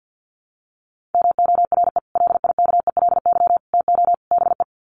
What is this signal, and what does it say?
Morse code, heard as a keyed tone, about 50 wpm, 700 Hz. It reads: MORE LIKELY TO BE
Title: Sets of 4 Words 50wpm